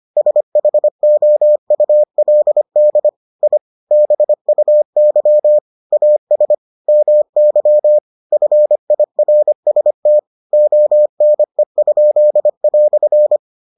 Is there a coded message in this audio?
Yes: Morse code from 0.2 to 13.4 s, heard as a keyed tone, about 25 wpm, 600 Hz. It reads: SHOULD I BUY AS MY FIRST ONE?"